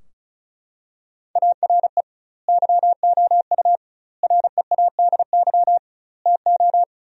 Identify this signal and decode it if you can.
Morse code, heard as a keyed tone, about 34 wpm, 700 Hz. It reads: ARE YOU READY TO
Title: 2x Word-Spacing - Sentences from Top 400 Words - 35wpm